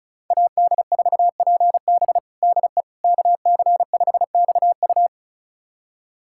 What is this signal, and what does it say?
Morse code, heard as a keyed tone, about 34 wpm, 690 Hz. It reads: AD4PB DE KC5XU
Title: USA Callsign DE Callsign - 5 Characters - 35wpm